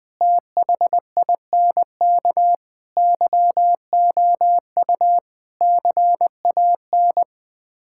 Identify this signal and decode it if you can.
Morse code, heard as a keyed tone, about 20 wpm, 700 Hz. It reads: THINK YOU CAN
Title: Build A Phrase - 5-Word Phrases - 20wpm